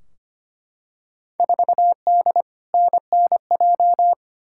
Morse code, heard as a keyed tone, about 25 wpm, 700 Hz. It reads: 4D NNJ
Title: Field Day Exchanges Encore - 25wpm